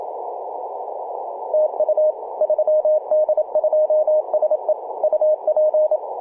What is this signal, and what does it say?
Morse code, heard as a keyed tone, about 27 wpm, 620 Hz, with radio noise. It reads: TU 3D2SE UP
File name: 3d2se-18-080mhz-cw